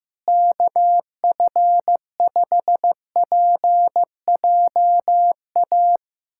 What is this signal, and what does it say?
Morse code, heard as a keyed tone, about 14 wpm, 700 Hz. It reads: KF5PJA
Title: USA Callsign DE Callsign - 6 Characters - 15wpm